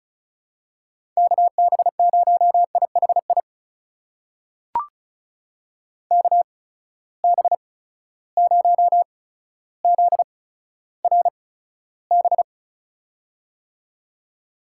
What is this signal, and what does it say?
Morse code, heard as a keyed tone, about 36 wpm, spaced as for 15 wpm, 700 Hz. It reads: KB0IHI KB0ZRB
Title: USA Call Signs - 6 Characters - 8x Character Spacing - 35wpm